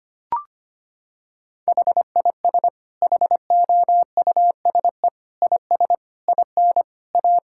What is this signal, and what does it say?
Morse code, heard as a keyed tone, about 25 wpm, 700 Hz. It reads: HIS HOUSE IS IN A
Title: Sentences from Top 1000 Words - Review Part 6 - 25wpm